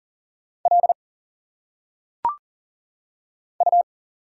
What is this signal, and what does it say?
Morse code, heard as a keyed tone, about 39 wpm, 700 Hz. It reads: L U
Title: Single Letters - Phonetic Pronunciation 40wpm